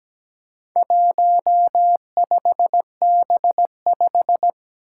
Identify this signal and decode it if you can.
Morse code, heard as a keyed tone, about 17 wpm, 700 Hz. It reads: 15B5
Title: Four Challenging Letter-Number Combinations 17wpm